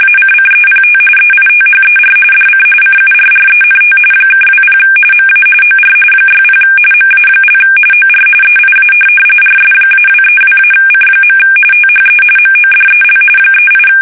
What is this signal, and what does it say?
RTTY, about 45.45 baud. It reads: ACCEPT EITHER DC CURRENT (20 MA OR 60 MA) OR EIA RS-232-C. THE MODEL 38
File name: fsk850 / teletype-corp